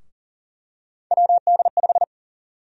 Morse code, heard as a keyed tone, about 40 wpm, 700 Hz. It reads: WD5
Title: USA Call Sign Prefixes - 40wpm